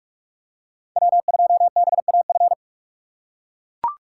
Morse code, heard as a keyed tone, about 43 wpm, 700 Hz. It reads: W2BAF
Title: USA Call Signs - 5 Characters - 4x Character Spacing - 45wpm